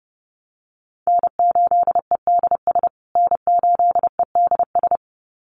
Morse code, heard as a keyed tone, about 30 wpm, 700 Hz. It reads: N8EDS N8EDS
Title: USA Call Signs Repeated 2x 30wpm